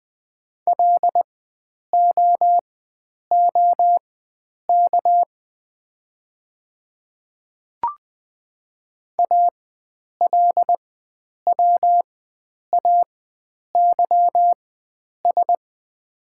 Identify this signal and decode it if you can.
Morse code, heard as a keyed tone, about 20 wpm, spaced as for 9 wpm, 700 Hz. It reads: LOOK ALWAYS
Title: 4x Character Spacing - Top 300 Words - 20wpm